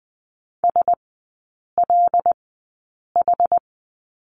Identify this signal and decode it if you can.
Morse code, heard as a keyed tone, about 20 wpm, spaced as for 8 wpm, 700 Hz. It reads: SLH
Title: 212 Lesson 34e - Number 8 - Three Characters 20wpm